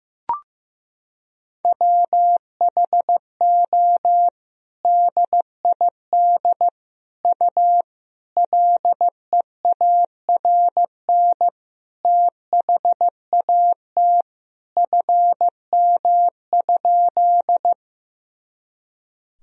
Morse code, heard as a keyed tone, about 15 wpm, 700 Hz. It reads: WHO DID U LEARN THAT FM?
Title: Sentences from Top 300 Words and Abbreviations 15wpm